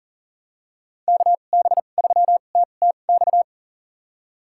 Morse code, heard as a keyed tone, about 40 wpm, spaced as for 29 wpm, 700 Hz. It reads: KB3TTX
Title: USA Call Signs - 6 Characters - 2x Character Spacing - 40wpm